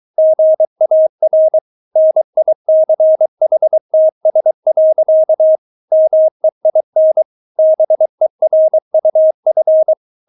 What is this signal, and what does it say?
Morse code, heard as a keyed tone, about 23 wpm, 630 Hz. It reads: GAR NICHTS. MEIN BERUF
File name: Witz4 23wpm